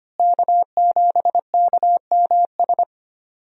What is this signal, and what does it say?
Morse code, heard as a keyed tone, about 25 wpm, 700 Hz. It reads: K7KMS
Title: USA Call Signs - 5 Characters - Repeated 3x - 25wpm